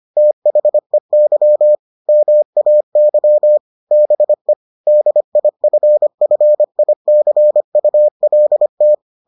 Morse code, heard as a keyed tone, about 25 wpm, 600 Hz. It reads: THEY MAY BE DIFFICULT